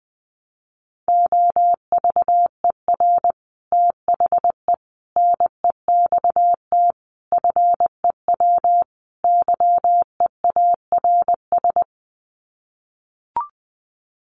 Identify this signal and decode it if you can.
Morse code, heard as a keyed tone, about 20 wpm, 700 Hz. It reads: OVER THE NEXT FEW YEARS
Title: Sets of 5 Words Encore 20wpm